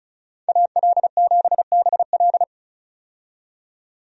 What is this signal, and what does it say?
Morse code, heard as a keyed tone, about 34 wpm, 700 Hz. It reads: AL7BL
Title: USA Call Signs - 5 Characters - 35wpm